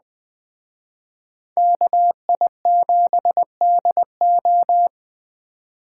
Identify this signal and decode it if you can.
Morse code, heard as a keyed tone, about 21 wpm, 700 Hz. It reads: KI7DO
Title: Speed Racing - USA 2x2 Call Signs 17wpm